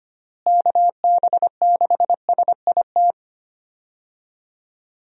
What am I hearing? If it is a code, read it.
Morse code, heard as a keyed tone, about 25 wpm, 700 Hz. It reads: KB6SIT
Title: USA Call Signs - 6 Characters - Repeated 3x - 25wpm